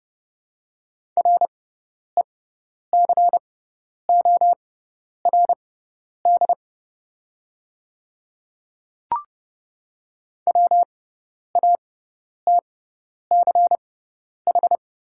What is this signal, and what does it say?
Morse code, heard as a keyed tone, about 30 wpm, spaced as for 10 wpm, 690 Hz. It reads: RECORD WATCH
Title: 6x Character Spacing - Top 500 Words - 30wpm